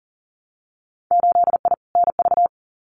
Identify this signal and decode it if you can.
Morse code, heard as a keyed tone, about 40 wpm, 700 Hz. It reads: 8I NV
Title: Winter Field Day Exchanges Encore 40wpm